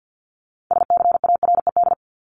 Morse code, heard as a keyed tone, about 50 wpm, 700 Hz. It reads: SCARED
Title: Top 501-750 Words 50wpm